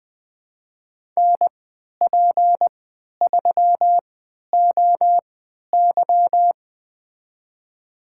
Morse code, heard as a keyed tone, about 21 wpm, spaced as for 11 wpm, 700 Hz. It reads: NP3OY
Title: USA Call Signs - 3x Character Spacing - 20wpm